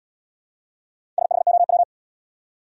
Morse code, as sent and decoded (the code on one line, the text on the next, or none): ... .... --- .--
SHOW